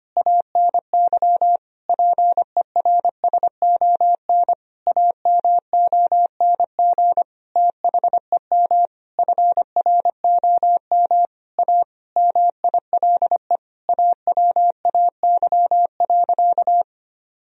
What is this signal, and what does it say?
Morse code, heard as a keyed tone, about 25 wpm, 700 Hz. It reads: ANY PERSON AMONG THEM FROM A MILE AWAY.
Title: Sentences from Top 1000 Words - Review Part 3 - 25wpm